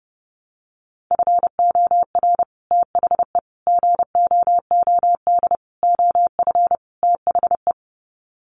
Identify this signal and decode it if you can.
Morse code, heard as a keyed tone, about 28 wpm, 700 Hz. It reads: FOR THE GOOD OF THE
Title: Sets of 5 Words Encore2 30wpm